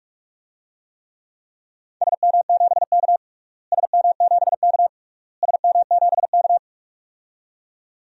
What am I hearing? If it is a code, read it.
Morse code, heard as a keyed tone, about 46 wpm, 700 Hz. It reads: SM7K SM7K SM7K
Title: DXCC-100 Call Signs - 4 Characters - Repeated 3x - 45wpm